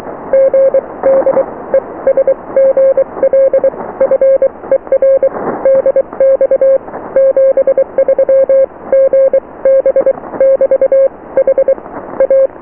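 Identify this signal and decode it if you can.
Morse code, heard as a keyed tone, about 23 wpm, spaced as for 19 wpm, 540 Hz. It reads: GD ES GL FER DX 73 GB <BT> HA
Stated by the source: Morse QSO on 14 MHz with a German station, received at RST 579.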